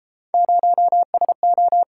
Morse code, heard as a keyed tone, about 33 wpm, 700 Hz. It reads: 0SO